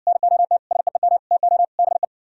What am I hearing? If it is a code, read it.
Morse code, heard as a keyed tone, about 58 wpm, 700 Hz. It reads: NOT SEEM TO BE